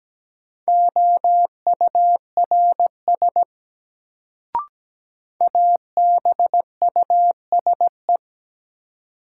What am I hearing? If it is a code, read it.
Morse code, heard as a keyed tone, about 17 wpm, 700 Hz. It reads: OURS ABUSE
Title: Sentences from Top 1300 Words - New Word Drill Alt - 17wpm